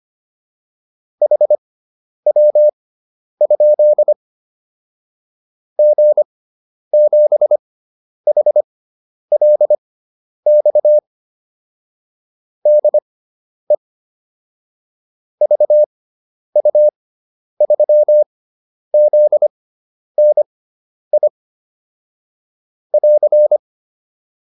Morse code, heard as a keyed tone, about 25 wpm, spaced as for 10 wpm, 600 Hz. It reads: HW? G7HLX DE VU3ZNI <AR>
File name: QSO 410 10